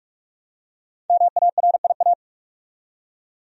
Morse code, heard as a keyed tone, about 44 wpm, 700 Hz. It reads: MARIA
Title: USA Common Names - 45wpm